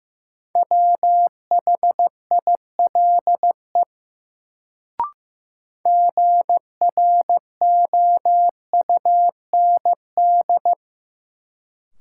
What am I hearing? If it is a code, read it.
Morse code, heard as a keyed tone, about 15 wpm, 700 Hz. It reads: WHILE GROUND
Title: Sets of 3 Words - New Word Drill - 15wpm